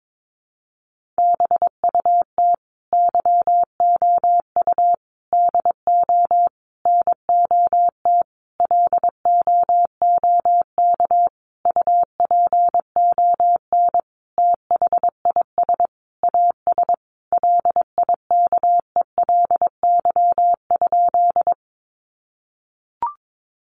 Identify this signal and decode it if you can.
Morse code, heard as a keyed tone, about 22 wpm, 700 Hz. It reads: BUT YOU DO NOT LOOK UPON THIS AS LIKELY?
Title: Sentences from Top 1100 Words 22wpm